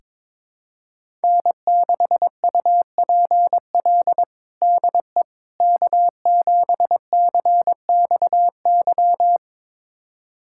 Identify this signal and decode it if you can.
Morse code, heard as a keyed tone, about 22 wpm, 700 Hz. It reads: N6UPL DE K7CXY
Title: USA Callsign DE Callsign - 5 Characters - 22wpm